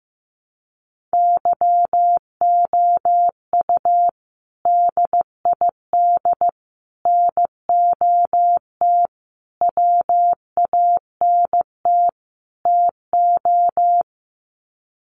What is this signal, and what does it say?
Morse code, heard as a keyed tone, about 15 wpm, 700 Hz. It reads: YOU DID NOT WANT TO
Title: Sets of 5 Words Encore2 15wpm